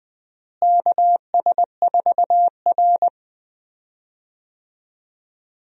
Morse code, heard as a keyed tone, about 20 wpm, 700 Hz. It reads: KS4R
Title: USA Call Signs - 4 Characters - Repeated 3x - 20wpm